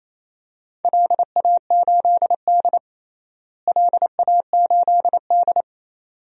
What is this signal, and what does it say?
Morse code, heard as a keyed tone, about 28 wpm, 700 Hz. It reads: LA8D LA8D
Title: DXCC-100 Call Signs - 4 Characters - Repeated 2x - 28wpm